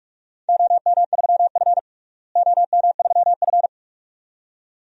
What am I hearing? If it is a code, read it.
Morse code, heard as a keyed tone, about 47 wpm, 700 Hz. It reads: OM3F OM3F
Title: DXCC-100 Call Signs - 4 Characters - Repeated 2x - 45wpm